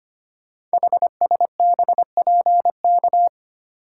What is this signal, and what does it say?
Morse code, heard as a keyed tone, about 25 wpm, 700 Hz. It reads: HSBPK
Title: Code Groups - 5 Letters - 25wpm